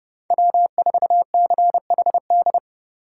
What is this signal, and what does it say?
Morse code, heard as a keyed tone, about 31 wpm, 700 Hz. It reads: W4CHD
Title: USA Call Signs - 5 Characters - 4x Character Spacing - 30wpm